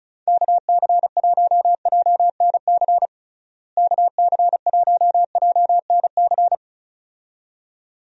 Morse code, heard as a keyed tone, about 36 wpm, 700 Hz. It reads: KC1JNC KC1JNC
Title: USA Call Signs - 6 Characters - Repeated 2x - 35wpm